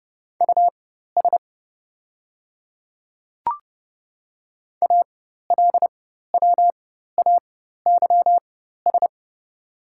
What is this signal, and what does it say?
Morse code, heard as a keyed tone, about 30 wpm, spaced as for 14 wpm, 700 Hz. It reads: US ALWAYS
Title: 4x Character Spacing - Top 300 Words - 30wpm